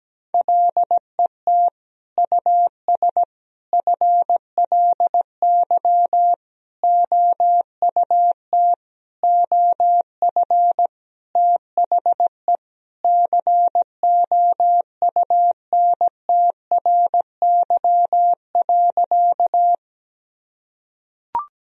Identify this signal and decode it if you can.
Morse code, heard as a keyed tone, about 17 wpm, 700 Hz. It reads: LET US FLY OUT OF THE COUNTRY.
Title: Sentences from Top 1000 Words - Review Part 2 - 17wpm